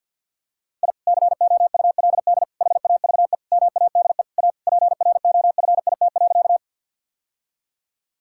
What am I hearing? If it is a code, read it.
Morse code, heard as a keyed tone, about 46 wpm, 700 Hz. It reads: I COULD HAVE MADE A PROFIT.
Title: Sentences from Top 1000 Words - Review Part 5 - 50wpm